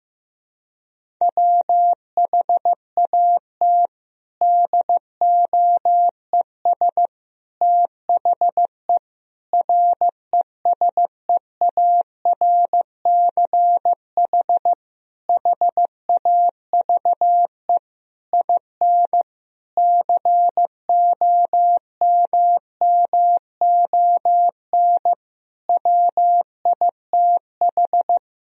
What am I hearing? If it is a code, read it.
Morse code, heard as a keyed tone, about 15 wpm, 700 Hz. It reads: WHAT DOES THE RESEARCH HAVE IN COMMON WITH
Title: Sentences from Top 1000 Words - Review Part 4 - 15wpm